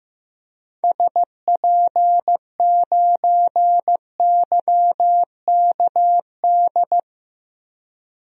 Morse code, heard as a keyed tone, about 15 wpm, 700 Hz. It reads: SP9YKD
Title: DXCC-100 Call Signs - 6 Characters - Repeated 2x - 15wpm